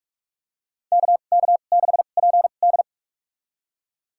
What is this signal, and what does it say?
Morse code, heard as a keyed tone, about 45 wpm, spaced as for 32 wpm, 700 Hz. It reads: KK6PD
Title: USA Call Signs - 2x Character Spacing - 45wpm